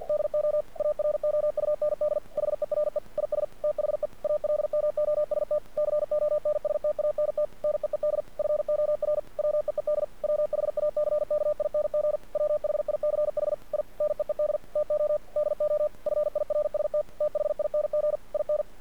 Morse code, heard as a keyed tone, about 48 wpm, 600 Hz. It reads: DO AROUND HERE IS THE ALMOST CONSTANT NEED FOR WEED WHACKING WHICH I NEED TO DO FIRST THING IN
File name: archived / CRSnet